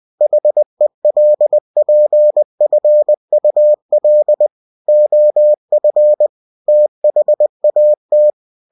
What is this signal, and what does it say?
Morse code, heard as a keyed tone, about 20 wpm, 600 Hz. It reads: HELPFUL OF THAT
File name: cw-books / winnie-the-pooh / winnie-20wpm-page-0034